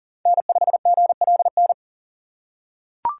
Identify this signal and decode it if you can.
Morse code, heard as a keyed tone, about 40 wpm, 700 Hz. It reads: N5GLN
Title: USA Call Signs - 5 Characters - 4x Character Spacing - 40wpm